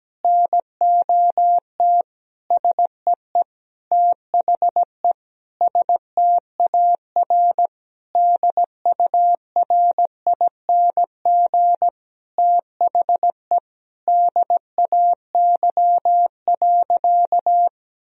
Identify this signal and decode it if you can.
Morse code, heard as a keyed tone, about 17 wpm, 700 Hz. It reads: NOT SEE THE STAR DURING THE DAY.
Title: Sentences from Top 1000 Words - Review Part 2 - 17wpm